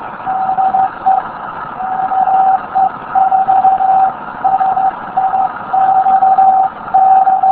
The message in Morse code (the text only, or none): none